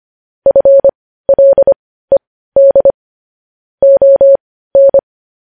Morse code, heard as a keyed tone, about 25 wpm, spaced as for 15 wpm, 550 Hz. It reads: FLED ON